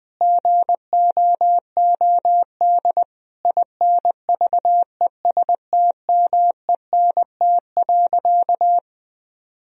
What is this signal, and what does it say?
Morse code, heard as a keyed tone, about 20 wpm, 700 Hz. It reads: GOOD INVESTMENT.
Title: Sentences from Top 1000 Words - Review Part 4 - 20wpm